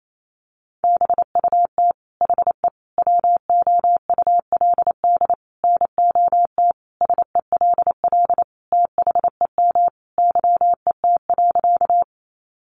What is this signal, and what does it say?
Morse code, heard as a keyed tone, about 28 wpm, 700 Hz. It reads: BUT HE WOULD NOT SELL THEM YET.
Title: Sentences from Top 1100 Words 28wpm